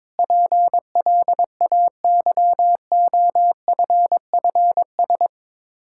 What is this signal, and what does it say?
Morse code, heard as a keyed tone, about 22 wpm, 700 Hz. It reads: PLAYOFFS
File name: practice / 22wpm / Sets of 7 Words - 22wpm